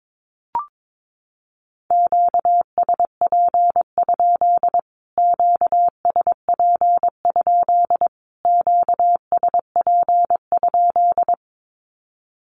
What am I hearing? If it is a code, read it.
Morse code, heard as a keyed tone, about 22 wpm, 700 Hz. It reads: QSP? QSP? QSP?
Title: Q-Codes Repeated 3x 22wpm